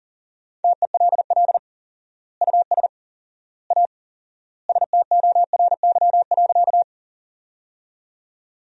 Morse code, heard as a keyed tone, about 37 wpm, 700 Hz. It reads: TELL US A STORY.
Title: 4x Word-Spacing - Sentences from Top 300 Words - 40wpm